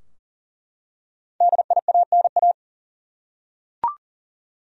Morse code, heard as a keyed tone, about 40 wpm, 700 Hz. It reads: DIANA
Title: USA Common Names - 40wpm